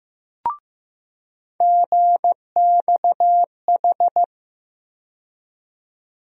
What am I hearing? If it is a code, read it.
Morse code, heard as a keyed tone, about 15 wpm, 700 Hz. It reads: GXH
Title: USA Call Sign Postfixes - 3 Letters 15wpm